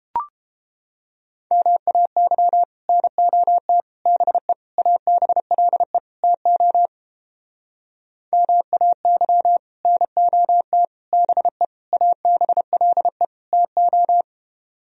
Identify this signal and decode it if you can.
Morse code, heard as a keyed tone, about 29 wpm, 700 Hz. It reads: MAY NOT BE ABLE TO MAY NOT BE ABLE TO